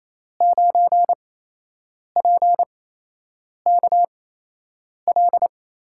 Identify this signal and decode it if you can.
Morse code, heard as a keyed tone, about 28 wpm, spaced as for 8 wpm, 700 Hz. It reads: 9PKL
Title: USA Call Signs - 8x Character Spacing - 28wpm